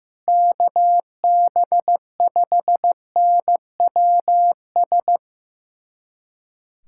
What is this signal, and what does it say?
Morse code, heard as a keyed tone, about 15 wpm, 700 Hz. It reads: KB5NWS
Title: USA Callsign DE Callsign 15wpm